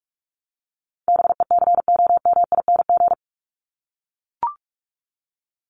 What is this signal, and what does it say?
Morse code, heard as a keyed tone, about 44 wpm, 700 Hz. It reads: BECOMING
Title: ING Suffix - Top 100 Words with Base Words 45wpm